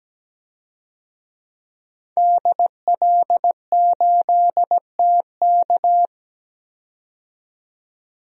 Morse code, heard as a keyed tone, about 17 wpm, 700 Hz. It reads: DL8TK
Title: DXCC-100 Call Signs - 5 Characters - 17wpm